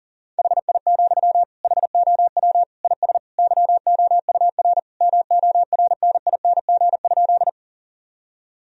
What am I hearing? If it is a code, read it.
Morse code, heard as a keyed tone, about 40 wpm, 700 Hz. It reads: HI, HOW IS YOUR MORNING?
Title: Sentences from Top 1000 Words - Review Part 3 - 40wpm